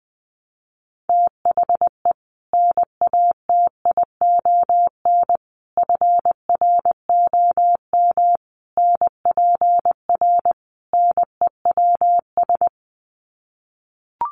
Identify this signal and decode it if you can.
Morse code, heard as a keyed tone, about 20 wpm, 700 Hz. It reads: THE NATION FROM NPR NEWS
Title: Sets of 5 Words 20wpm